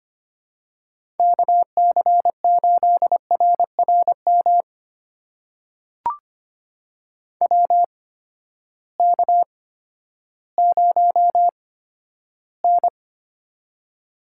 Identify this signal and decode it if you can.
Morse code, heard as a keyed tone, about 25 wpm, spaced as for 12 wpm, 700 Hz. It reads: KC8RRM WK0N
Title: USA Call Signs - 8x Character Spacing - 25wpm